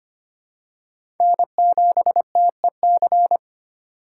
Morse code, heard as a keyed tone, about 25 wpm, 700 Hz. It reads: N7TEC
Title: USA Call Signs - 5 Characters - 4x Character Spacing - 25wpm